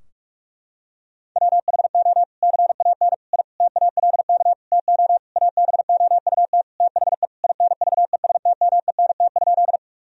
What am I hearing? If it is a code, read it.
Morse code, heard as a keyed tone, about 46 wpm, 700 Hz. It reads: WHO CAN I TALK TO ABOUT THE INVESTMENT?
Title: Sentences from Top 1000 Words - Review Part 4 - 45wpm